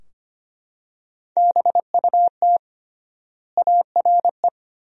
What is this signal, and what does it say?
Morse code, heard as a keyed tone, about 25 wpm, 700 Hz. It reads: BUT ARE
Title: 3x Word-Spacing - Sentences from Top 300 Words - 25wpm